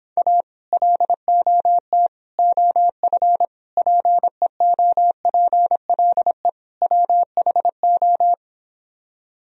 Morse code, heard as a keyed tone, about 26 wpm, 700 Hz. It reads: A LOT OF PEOPLE WHO